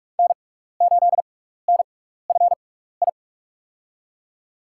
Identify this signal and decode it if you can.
Morse code, heard as a keyed tone, about 44 wpm, spaced as for 15 wpm, 690 Hz. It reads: N8NFI
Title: USA Call Signs - 5 Characters - 6x Character Spacing - 45wpm